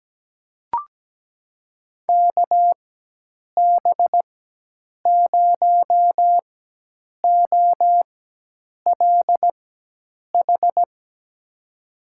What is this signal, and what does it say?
Morse code, heard as a keyed tone, about 17 wpm, spaced as for 8 wpm, 700 Hz. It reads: KB0OLH
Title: USA Call Signs - 4x Character Spacing - 17wpm